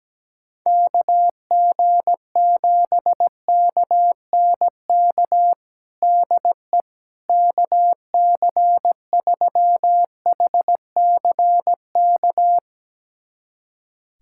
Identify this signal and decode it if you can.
Morse code, heard as a keyed tone, about 17 wpm, 700 Hz. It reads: KG7KNK DE KC3HCK
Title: USA Callsign DE Callsign - 6 Characters - 17wpm